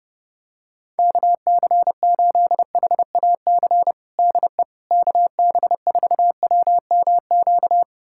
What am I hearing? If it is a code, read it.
Morse code, heard as a keyed tone, about 30 wpm, 700 Hz. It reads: KC8HAC DE KB4WMQ
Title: USA Callsign DE Callsign - 6 Characters - 30wpm